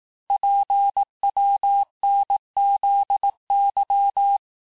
Morse code, heard as a keyed tone, about 18 wpm, 800 Hz. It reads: PWNZY
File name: cours-du-lundi files / cours-1594--teste-chiffre-vitesse-17mpm